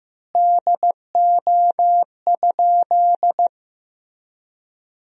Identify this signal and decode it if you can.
Morse code, heard as a keyed tone, about 15 wpm, 700 Hz. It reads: DO?
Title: 3x Word-Spacing - Sentences from Top 100 Words - 15wpm